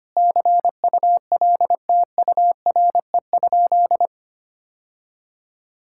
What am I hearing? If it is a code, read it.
Morse code, heard as a keyed tone, about 25 wpm, 700 Hz. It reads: CULTURE?
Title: Sentences from Top 1000 Words - Review Part 6 - 25wpm